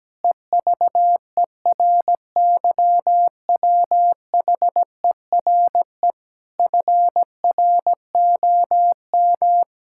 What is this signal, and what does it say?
Morse code, heard as a keyed tone, about 17 wpm, 700 Hz. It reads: EVERYWHERE FROM
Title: Sentences from Top 1600 Words 17wpm